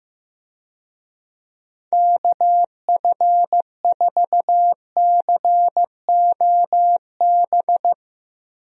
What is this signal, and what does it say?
Morse code, heard as a keyed tone, about 15 wpm, 700 Hz. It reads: KF4COB
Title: USA Call Signs - 6 Characters - 15wpm